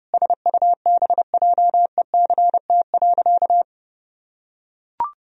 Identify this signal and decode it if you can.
Morse code, heard as a keyed tone, about 30 wpm, 700 Hz. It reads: SUBJECT.
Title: Sentences from Top 1000 Words - Review Part 4 - 30wpm